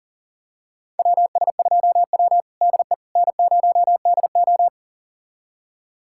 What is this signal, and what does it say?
Morse code, heard as a keyed tone, about 38 wpm, 700 Hz. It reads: WS2W DE N0DO
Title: USA Callsign DE Callsign - 4 Characters - 40wpm